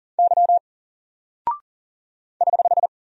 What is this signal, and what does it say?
Morse code, heard as a keyed tone, about 39 wpm, 700 Hz. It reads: Y <HH>
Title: Single Character (Letter:Number:Punctuation:Prosign) 40wpm